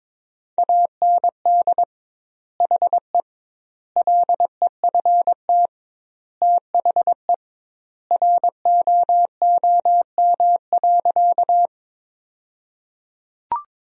Morse code, heard as a keyed tone, about 22 wpm, 700 Hz. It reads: AND HE LEFT THE ROOM.
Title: 2x Word-Spacing - Sentences from Top 300 Words - 22wpm